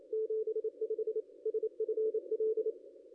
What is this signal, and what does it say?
Morse code, heard as a keyed tone, about 28 wpm, 440 Hz. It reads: 75 SFL
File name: contests / ss-cw-k5tr-op-n5rz -0500